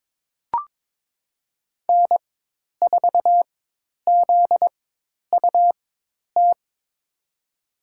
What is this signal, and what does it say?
Morse code, heard as a keyed tone, about 22 wpm, spaced as for 10 wpm, 700 Hz. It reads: N4ZUT
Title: USA Call Signs - 5 Characters - 4x Character Spacing - 22wpm